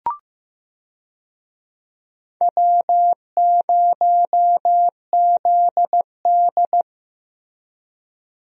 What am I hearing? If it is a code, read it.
Morse code, heard as a keyed tone, about 15 wpm, 700 Hz. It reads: W0ZD
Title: USA Call Signs - 4 Characters - Repeated 2x - 15wpm